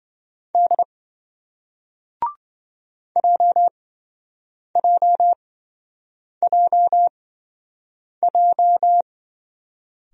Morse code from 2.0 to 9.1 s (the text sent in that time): JJJJ